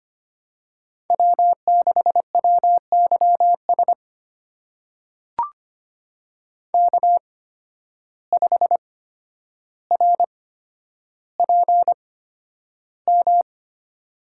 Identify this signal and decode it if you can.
Morse code, heard as a keyed tone, about 25 wpm, spaced as for 11 wpm, 700 Hz. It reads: W6WYS K5RPM
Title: USA Call Signs - 5 Characters - 8x Character Spacing - 25wpm